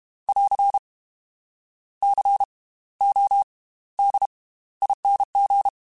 Morse code, heard as a keyed tone, about 32 wpm, spaced as for 17 wpm, 800 Hz. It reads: <AR> CODING
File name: Slow ing @30